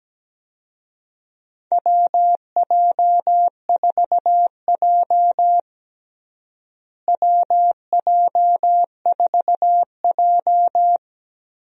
Morse code, heard as a keyed tone, about 17 wpm, 700 Hz. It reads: WJ4J WJ4J
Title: USA Call Signs - 4 Characters - Repeated 3x - 17wpm